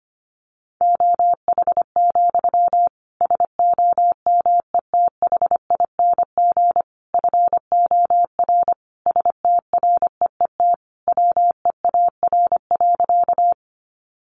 Morse code, heard as a keyed tone, about 25 wpm, 700 Hz. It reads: OH, SOMETHING FOR STREET WEAR.
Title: Sentences from Top 1500 Words 25wpm